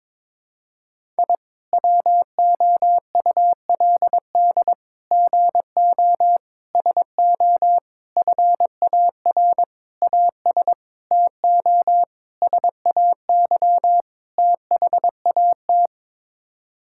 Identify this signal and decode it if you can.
Morse code, heard as a keyed tone, about 22 wpm, 700 Hz. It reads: I WOULD GO SO FAR AS TO SAY THAT
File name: Sets of 9 Words - 22wpm